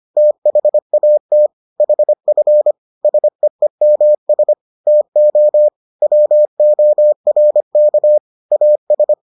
Morse code, heard as a keyed tone, about 25 wpm, 600 Hz. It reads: THAT HF SEEMS TO WORK AS